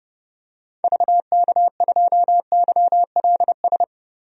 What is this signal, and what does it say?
Morse code, heard as a keyed tone, about 31 wpm, 700 Hz. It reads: VK2YLS
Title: DXCC-100 Call Signs - 6 Characters - 30wpm